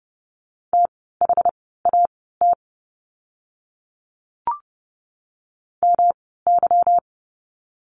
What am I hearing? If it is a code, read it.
Morse code, heard as a keyed tone, about 30 wpm, spaced as for 17 wpm, 700 Hz. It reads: THAT MY
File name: course / 30:10wpm / 230 Lesson 37f - Letter x - Words 30wpm